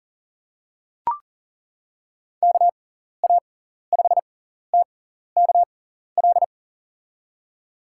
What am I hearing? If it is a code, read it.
Morse code, heard as a keyed tone, about 40 wpm, spaced as for 14 wpm, 700 Hz. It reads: KA5TKL
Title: USA Call Signs - 6 Characters - 6x Character Spacing - 40wpm